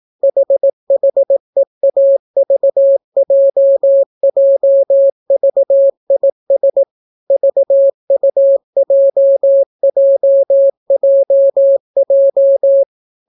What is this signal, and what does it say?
Morse code, heard as a keyed tone, about 18 wpm, 550 Hz. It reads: HHEAVJJVIS VUJJJJ